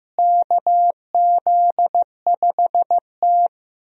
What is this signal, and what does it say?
Morse code, heard as a keyed tone, about 15 wpm, 700 Hz. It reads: KZ5T
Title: USA Call Signs - 4 Characters - Repeated 2x - 15wpm